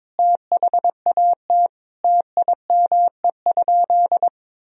Morse code, heard as a keyed tone, about 22 wpm, 700 Hz. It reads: THAT TIME?
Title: Sentences from Top 200 Words and Abbreviations 22wpm